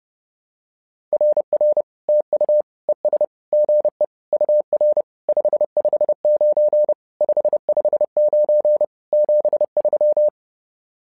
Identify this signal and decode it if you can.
Morse code, heard as a keyed tone, about 30 wpm, 600 Hz. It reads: RR TU ES GE UR 559 559 73
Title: SOTA QSO Practice 30wpm